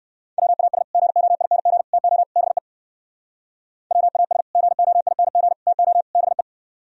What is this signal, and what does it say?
Morse code, heard as a keyed tone, about 58 wpm, 700 Hz. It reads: WAS GOING TO BE WAS GOING TO BE